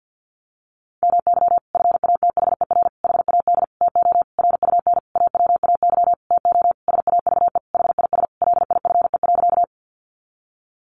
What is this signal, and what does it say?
Morse code, heard as a keyed tone, about 50 wpm, 700 Hz. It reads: MY FATHER HAD TO RUN AWAY TO SAVE HIS LIFE.
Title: Sentences from Top 1100 Words 50wpm